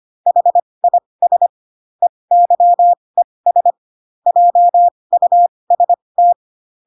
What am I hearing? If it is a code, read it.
Morse code, heard as a keyed tone, about 25 wpm, spaced as for 20 wpm, 700 Hz. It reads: HIS EYES JUST
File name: Pinocchio / Pinocchio ch6 15 25 ik1hge